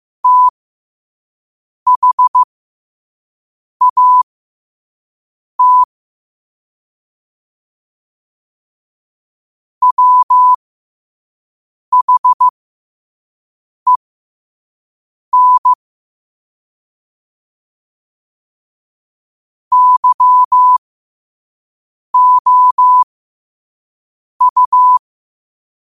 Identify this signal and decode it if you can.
Morse code, heard as a keyed tone, about 15 wpm, spaced as for 5 wpm, 1000 Hz. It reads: THAT WHEN YOU
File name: 5wpm